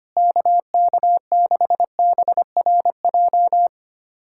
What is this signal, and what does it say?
Morse code, heard as a keyed tone, about 25 wpm, 700 Hz. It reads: KK6BRJ
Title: USA Callsign DE Callsign - 6 Characters - 25wpm